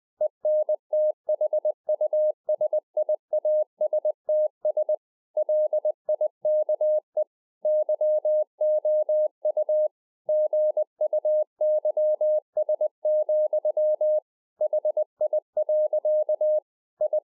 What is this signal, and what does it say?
Morse code, heard as a keyed tone, about 20 wpm, 610 Hz. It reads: ENTHUSIASTS LIKE YOU GUYS, HI. I
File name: 2011-04-mp3 / A SMALL TRANCEIVER 3 20WPM